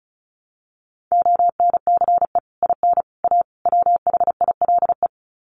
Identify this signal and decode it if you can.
Morse code, heard as a keyed tone, about 34 wpm, 700 Hz. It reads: ONCE IN A WHILE
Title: Sets of 4 Words 35wpm